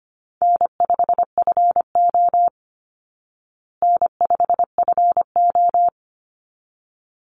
Morse code, heard as a keyed tone, about 25 wpm, 700 Hz. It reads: N5FO N5FO
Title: Call Signs Repeated 3x - 25wpm